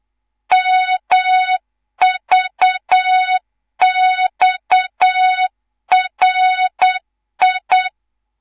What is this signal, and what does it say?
Morse code, heard as a keyed tone, about 8 wpm, 3700 Hz, with noise. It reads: MVXRI
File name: Morze-HA2MN/hang/11 GYAKORLAT